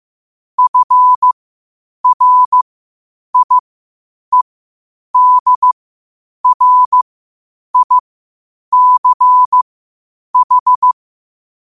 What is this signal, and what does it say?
Morse code, heard as a keyed tone, about 15 wpm, spaced as for 8 wpm, 1000 Hz. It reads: FRIEDRICH
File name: qotdfiles / 8wpm